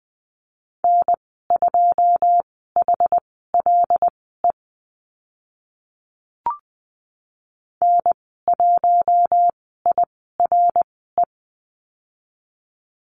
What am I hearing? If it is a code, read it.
Morse code, heard as a keyed tone, about 20 wpm, spaced as for 14 wpm, 700 Hz. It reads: N2HLE N1IRE
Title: 088 Lesson 15g - Number 5 - Call Signs 20wpm